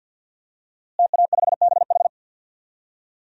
Morse code, heard as a keyed tone, about 47 wpm, 700 Hz. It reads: TA5BH
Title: DXCC-100 Call Signs - 5 Characters - Repeated 2x - 50wpm